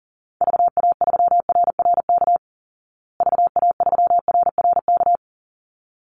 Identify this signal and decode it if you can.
Morse code, heard as a keyed tone, about 40 wpm, 700 Hz. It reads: VA3RRK VA3RRK
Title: Call Signs Repeated 3x - 40wpm